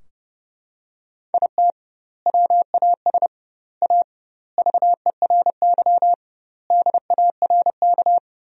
Morse code, heard as a keyed tone, about 32 wpm, 700 Hz. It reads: IT WAS A VERY DARK
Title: 2x Word-Spacing - Sentences from Top 500 Words - 30wpm